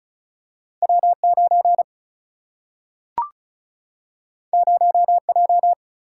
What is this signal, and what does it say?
Morse code, heard as a keyed tone, about 36 wpm, 700 Hz. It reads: W9 0J
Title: Single-Letter Single-Number 35wpm